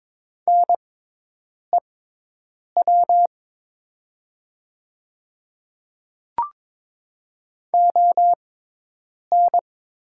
Morse code, heard as a keyed tone, about 22 wpm, spaced as for 8 wpm, 700 Hz. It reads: NEW ON
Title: 6x Character Spacing - Top 200 Words - 22wpm